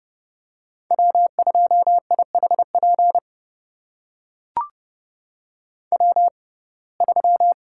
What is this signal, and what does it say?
Morse code, heard as a keyed tone, about 30 wpm, spaced as for 22 wpm, 700 Hz. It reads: W2IHP W3
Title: USA Call Signs - 5 Characters - 6x Character Spacing - 30wpm